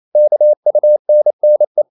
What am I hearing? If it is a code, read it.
Morse code, heard as a keyed tone, about 28 wpm, 600 Hz. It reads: KUNNE
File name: Ferie QRQ 28wpm